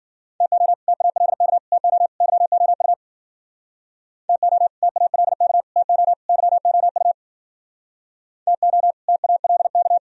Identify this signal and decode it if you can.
Morse code, heard as a keyed tone, about 51 wpm, 700 Hz. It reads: TO TALK TO YOU TO TALK TO YOU TO TALK